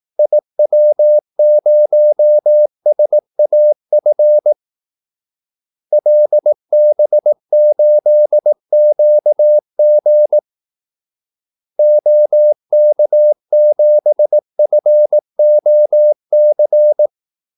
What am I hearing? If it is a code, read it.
Morse code, heard as a keyed tone, about 18 wpm, 600 Hz. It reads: IW0SAF LB8QG OK7FOC
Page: Kaldesignaler 18 wpm